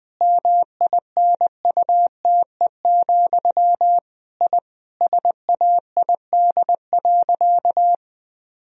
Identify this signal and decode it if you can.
Morse code, heard as a keyed tone, about 20 wpm, 700 Hz. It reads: MINUTE, I SAID.
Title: Sentences from Top 1000 Words - Review Part 2 - 20wpm